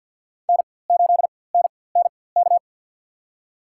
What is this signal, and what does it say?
Morse code, heard as a keyed tone, about 51 wpm, spaced as for 23 wpm, 700 Hz. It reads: N8NNK
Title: USA Call Signs - 5 Characters - 4x Character Spacing - 50wpm